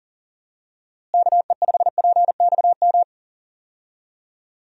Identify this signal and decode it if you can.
Morse code, heard as a keyed tone, about 40 wpm, 700 Hz. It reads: KE5PXM
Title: USA Call Signs - 6 Characters - Repeated 3x - 40wpm